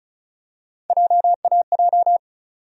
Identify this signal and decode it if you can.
Morse code, heard as a keyed tone, about 36 wpm, 700 Hz. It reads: JAJ
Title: USA Call Sign Postfixes - 3 Letters 35wpm